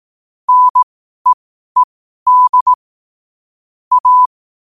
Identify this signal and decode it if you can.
Morse code, heard as a keyed tone, about 18 wpm, spaced as for 12 wpm, 1000 Hz. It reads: NEED A
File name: qotdfiles / 12wpm